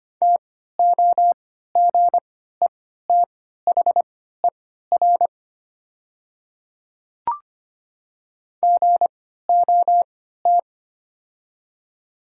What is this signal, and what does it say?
Morse code, heard as a keyed tone, about 25 wpm, spaced as for 14 wpm, 700 Hz. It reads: TOGETHER GOT
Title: 3x Character Spacing - Top 300 Words - 25wpm